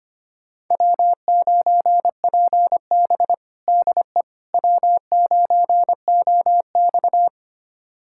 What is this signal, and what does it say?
Morse code, heard as a keyed tone, about 25 wpm, 700 Hz. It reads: W9PB DE W9OX
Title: USA Callsign DE Callsign - 4 Characters - 25wpm